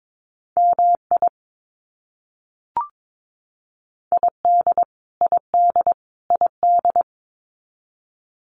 Morse code, heard as a keyed tone, about 22 wpm, 700 Hz. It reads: MI ID ID ID
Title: US States Repeated 3x 22wpm